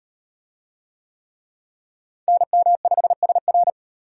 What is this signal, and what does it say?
Morse code, heard as a keyed tone, about 34 wpm, 700 Hz. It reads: NM5SR